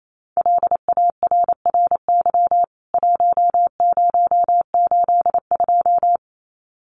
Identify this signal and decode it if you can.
Morse code, heard as a keyed tone, about 28 wpm, 700 Hz. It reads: LARRY 1082
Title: CWT Exchange 28wpm